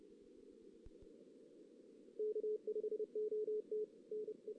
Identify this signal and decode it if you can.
Morse code, heard as a keyed tone, about 30 wpm, 440 Hz. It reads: K5OT NE
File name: 2016 JAN CW NAQP / 2016 Jan naqp k5ot - 0-4